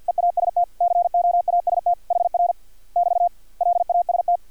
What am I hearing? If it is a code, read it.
Morse code, heard as a keyed tone, about 50 wpm, 700 Hz. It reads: EAST COAST HR <BT> LAST
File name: archived / CRSnet